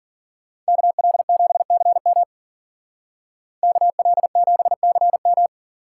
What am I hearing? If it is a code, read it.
Morse code, heard as a keyed tone, about 48 wpm, 700 Hz. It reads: KL7CM KL7CM
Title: Speed Racing - USA 2x2 Call Signs 40wpm